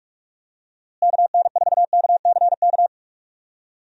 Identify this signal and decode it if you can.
Morse code, heard as a keyed tone, about 42 wpm, 700 Hz. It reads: KN4KCK
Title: USA Call Signs - 6 Characters - Repeated 2x - 45wpm